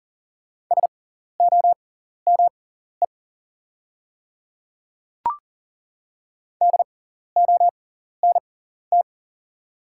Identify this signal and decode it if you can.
Morse code, heard as a keyed tone, about 40 wpm, spaced as for 14 wpm, 690 Hz. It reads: SOME DONT
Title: 6x Character Spacing - Top 300 Words - 40wpm